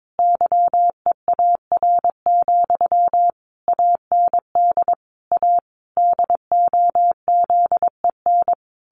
Morse code, heard as a keyed tone, about 22 wpm, 700 Hz. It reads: YEAR, AND A DOZEN
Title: A Sound Of Thunder 22wpm